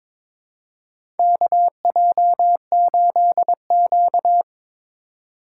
Morse code, heard as a keyed tone, about 22 wpm, 700 Hz. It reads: KJ8Q
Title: USA Call Signs - 3x Character Spacing - 22wpm